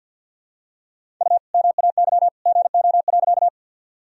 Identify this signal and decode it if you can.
Morse code, heard as a keyed tone, about 51 wpm, 700 Hz. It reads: U MAY GO.
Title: Sentences from Top 100 Words and Abbreviations 50wpm